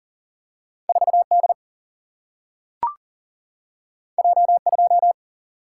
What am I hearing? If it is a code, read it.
Morse code, heard as a keyed tone, about 40 wpm, 700 Hz. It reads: 4 D J 2
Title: Single-Letter Single-Number 40wpm